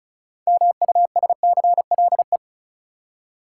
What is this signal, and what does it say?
Morse code, heard as a keyed tone, about 36 wpm, 700 Hz. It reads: MUSCLE
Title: Six Letter Words - 35wpm